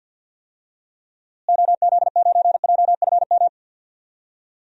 Morse code, heard as a keyed tone, about 52 wpm, 700 Hz. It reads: OZ9JFM
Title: DXCC-100 Call Signs - 6 Characters - Repeated 2x - 50wpm